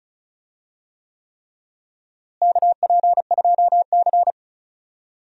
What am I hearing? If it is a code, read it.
Morse code, heard as a keyed tone, about 34 wpm, 700 Hz. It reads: KP2C
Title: DXCC-100 Call Signs - 4 Characters - Repeated 3x - 35wpm